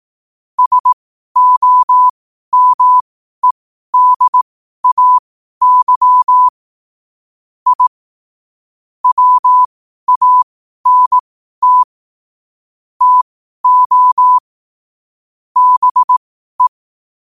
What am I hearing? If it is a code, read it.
Morse code, heard as a keyed tone, about 18 wpm, spaced as for 12 wpm, 1000 Hz. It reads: SOMEDAY I WANT TO BE
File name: qotdfiles / 12wpm